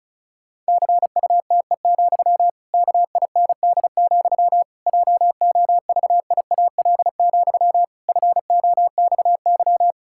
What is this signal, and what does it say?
Morse code, heard as a keyed tone, about 36 wpm, 700 Hz. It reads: CUTE, KIND, JOVIAL, FOXY
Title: Pangram Challenge v2 - 35wpm